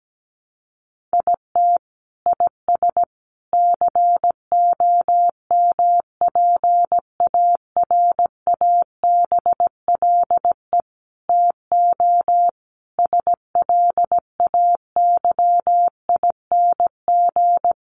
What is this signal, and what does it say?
Morse code, heard as a keyed tone, about 17 wpm, 700 Hz. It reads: IT IS COMPARABLE TO SLAYING
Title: A Sound Of Thunder 17wpm